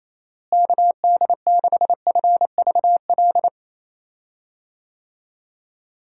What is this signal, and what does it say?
Morse code, heard as a keyed tone, about 28 wpm, 700 Hz. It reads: KD6FVL
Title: USA Call Signs - 6 Characters - Repeated 2x - 28wpm